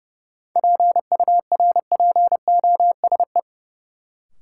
Morse code, heard as a keyed tone, about 30 wpm, 700 Hz. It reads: PURPOSE
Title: Sentences from Top 800 Words - New Word Drill - 30wpm